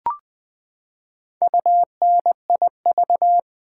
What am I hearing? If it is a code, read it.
Morse code, heard as a keyed tone, about 20 wpm, 700 Hz. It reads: UNIV
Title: Build A Word - 4 Letters At A Time - 10-Letter Words - 20wpm